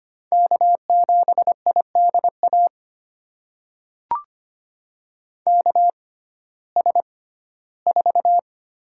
Morse code, heard as a keyed tone, about 25 wpm, spaced as for 15 wpm, 700 Hz. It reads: K7IDA KS4
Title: USA Call Signs - 5 Characters - 6x Character Spacing - 25wpm